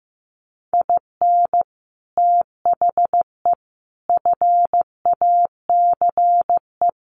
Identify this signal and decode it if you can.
Morse code, heard as a keyed tone, about 15 wpm, 700 Hz. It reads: IN THE FACE
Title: Sets of 5 Words 15wpm